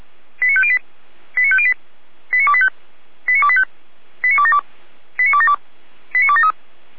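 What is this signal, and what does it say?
Five-tone selective call (ZVEI), heard as quick runs of five tones, tones of about 70 ms. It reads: 89508 89508 89285 89285 89272 89272 89273
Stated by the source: ZVEI five-tone call 89262 received by SDR.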